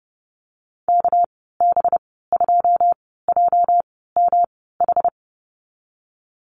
Morse code, heard as a keyed tone, about 32 wpm, spaced as for 17 wpm, 700 Hz. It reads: KB2JMH
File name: Lesson 37g - Letter x - Call Signs 30wpm